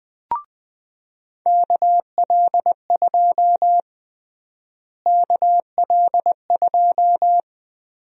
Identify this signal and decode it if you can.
Morse code, heard as a keyed tone, about 20 wpm, 700 Hz. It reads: KL2 KL2
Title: USA Call Sign Prefixes - Repeated 2x - 20wpm